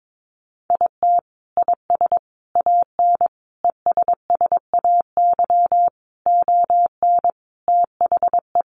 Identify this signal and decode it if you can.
Morse code, heard as a keyed tone, about 22 wpm, 700 Hz. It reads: IT IS AN ESSAY ON THE
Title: Sentences from Top 1500 Words 22wpm